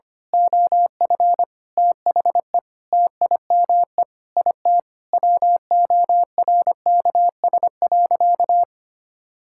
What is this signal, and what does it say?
Morse code, heard as a keyed tone, about 25 wpm, 700 Hz. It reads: OF THE TIME IT WORKS.
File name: practice / 25wpm / Sentences with 2-Word Phrases - 25wpm